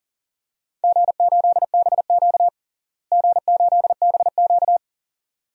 Morse code, heard as a keyed tone, about 40 wpm, 700 Hz. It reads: G8BQ G8BQ
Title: DXCC-100 Call Signs - 4 Characters - Repeated 2x - 40wpm